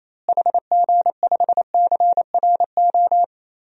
Morse code, heard as a keyed tone, about 28 wpm, 700 Hz. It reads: HG5CRO
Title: DXCC-100 Call Signs - 6 Characters - Repeated 3x - 28wpm